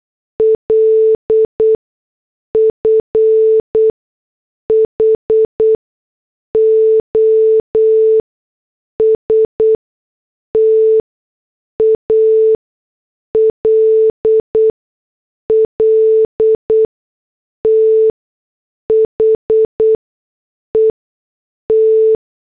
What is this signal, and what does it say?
Morse code, heard as a keyed tone, about 8 wpm, spaced as for 6 wpm, 440 Hz. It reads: LF HOST ALL THE T